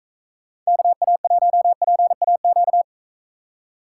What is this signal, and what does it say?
Morse code, heard as a keyed tone, about 45 wpm, 700 Hz. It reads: KA1PAQ